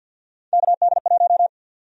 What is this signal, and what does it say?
Morse code, heard as a keyed tone, about 50 wpm, 700 Hz. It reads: KD1
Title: USA Call Sign Prefixes - Repeated 2x - 50wpm